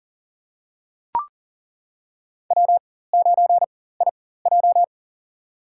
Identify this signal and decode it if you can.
Morse code, heard as a keyed tone, about 40 wpm, spaced as for 19 wpm, 700 Hz. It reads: W9IJ
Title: USA Call Signs - 4 Characters - 4x Character Spacing - 40wpm